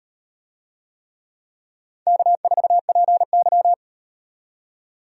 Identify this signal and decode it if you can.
Morse code, heard as a keyed tone, about 35 wpm, 700 Hz. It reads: K4PY